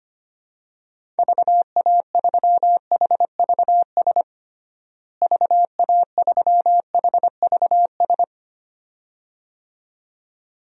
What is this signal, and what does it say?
Morse code, heard as a keyed tone, about 25 wpm, 700 Hz. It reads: VA3HVS VA3HVS
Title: DXCC-100 Call Signs - 6 Characters - Repeated 2x - 25wpm